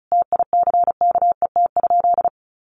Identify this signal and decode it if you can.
Morse code, heard as a keyed tone, about 34 wpm, 700 Hz. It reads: TICKET?
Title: Sentences from Top 1400 Words 35wpm